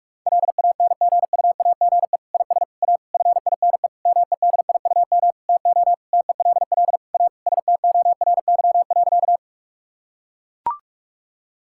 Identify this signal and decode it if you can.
Morse code, heard as a keyed tone, about 41 wpm, 700 Hz. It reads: LANGUAGE IS A FINE MEDIUM TO TELL A STORY.
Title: Sentences from Top 1000 Words - Review Part 6 - 45wpm